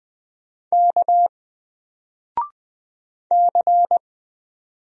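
Morse code, heard as a keyed tone, about 20 wpm, 700 Hz. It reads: K C
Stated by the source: Single letter and number Morse drill at 20 wpm, from N4LCD.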